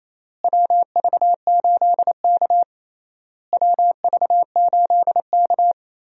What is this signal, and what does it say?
Morse code, heard as a keyed tone, about 28 wpm, 700 Hz. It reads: WV8K WV8K
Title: USA Call Signs - 4 Characters - Repeated 3x - 28wpm